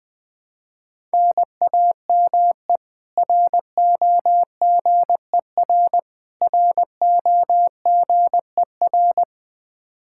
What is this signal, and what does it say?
Morse code, heard as a keyed tone, about 20 wpm, 700 Hz. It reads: NAME ROGER ROGER
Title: Practice QSO - Exchange 1 - Beginning - 20wpm